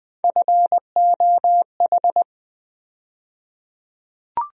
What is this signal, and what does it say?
Morse code, heard as a keyed tone, about 20 wpm, 700 Hz. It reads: FOH
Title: USA Call Sign Postfixes - 3 Letters 20wpm